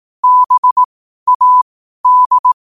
Morse code, heard as a keyed tone, about 18 wpm, spaced as for 12 wpm, 1000 Hz. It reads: BAD